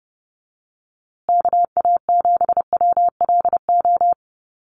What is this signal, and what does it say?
Morse code, heard as a keyed tone, about 30 wpm, 700 Hz. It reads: KA7WLO
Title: USA Call Signs Repeated 2x 30wpm